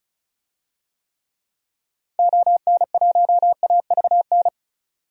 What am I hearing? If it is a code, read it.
Morse code, heard as a keyed tone, about 36 wpm, 700 Hz. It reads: ON1AVN
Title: DXCC-100 Call Signs - 6 Characters - 35wpm